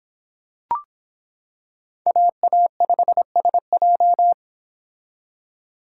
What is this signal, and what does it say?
Morse code, heard as a keyed tone, about 25 wpm, 700 Hz. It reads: AA5SJ